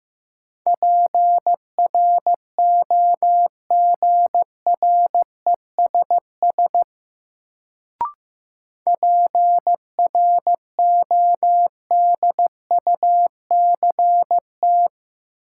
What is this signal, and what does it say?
Morse code, heard as a keyed tone, about 15 wpm, 700 Hz. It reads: PROGRESS PRODUCT
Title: Sentences from Top 1400 Words - New Word Drill Alt - 15wpm